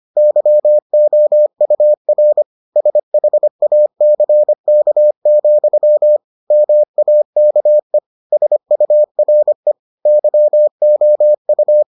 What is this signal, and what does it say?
Morse code, heard as a keyed tone, about 25 wpm, 600 Hz. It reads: YOUR SHACK, MAKE SURE YOU